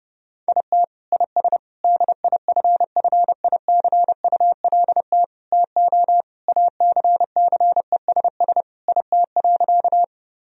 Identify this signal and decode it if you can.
Morse code, heard as a keyed tone, about 30 wpm, 700 Hz. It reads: IT IS DIFFICULT TO ACCESS IT.
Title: Sentences from Top 1000 Words - Review Part 4 - 30wpm